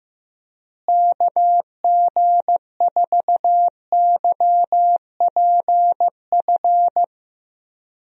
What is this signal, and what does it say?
Morse code, heard as a keyed tone, about 15 wpm, 700 Hz. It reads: KG4YPF
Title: USA Call Signs - 6 Characters - Repeated 3x - 15wpm